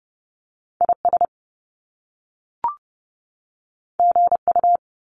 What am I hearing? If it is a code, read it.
Morse code, heard as a keyed tone, about 30 wpm, 700 Hz. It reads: IS GU
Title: Top Two Letter Combinations in Words 30wpm